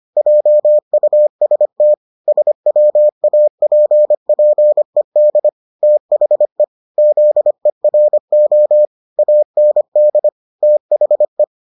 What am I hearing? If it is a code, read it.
Morse code, heard as a keyed tone, about 25 wpm, 600 Hz. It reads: JUST SWAPPED THE ZERO AND THE